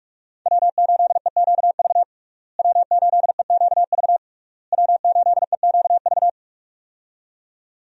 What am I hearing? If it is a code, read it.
Morse code, heard as a keyed tone, about 42 wpm, 700 Hz. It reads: W8EQV W8EQV W8EQV
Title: USA Call Signs - 5 Characters - Repeated 3x - 45wpm